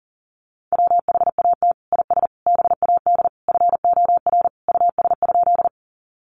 Morse code, heard as a keyed tone, about 40 wpm, 700 Hz. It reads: WHAT IS BAD FOR US?
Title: Sentences from Top 700 Words 40wpm